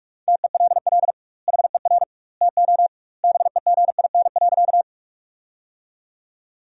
Morse code, heard as a keyed tone, about 46 wpm, 700 Hz. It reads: TELL HER TO BEGIN.
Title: 2x Word-Spacing - Sentences from Top 300 Words - 45wpm